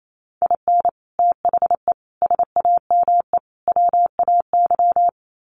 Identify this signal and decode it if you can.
Morse code, heard as a keyed tone, about 28 wpm, 700 Hz. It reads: IN THE SAME WAY
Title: Sets of 4 Words 28wpm